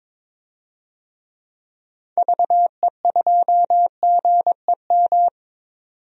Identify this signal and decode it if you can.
Morse code, heard as a keyed tone, about 22 wpm, 700 Hz. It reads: VE2GEM
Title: DXCC-100 Call Signs - 6 Characters - 22wpm